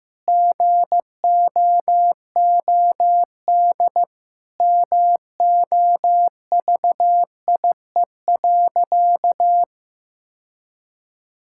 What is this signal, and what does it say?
Morse code, heard as a keyed tone, about 15 wpm, 700 Hz. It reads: GOOD MOVIE.
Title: Sentences from Top 1000 Words - Review Part 5 - 15wpm